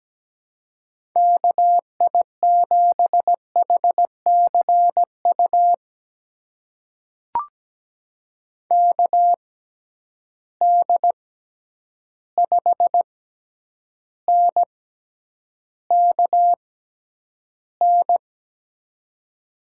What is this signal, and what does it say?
Morse code, heard as a keyed tone, about 17 wpm, spaced as for 9 wpm, 700 Hz. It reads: KI7HCU KD5NKN
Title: USA Call Signs - 6 Characters - 6x Character Spacing - 17wpm